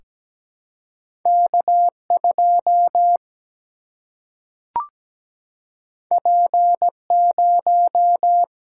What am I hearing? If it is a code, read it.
Morse code, heard as a keyed tone, about 17 wpm, 700 Hz. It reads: K 2 P 0
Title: Single-Letter Single-Number 17wpm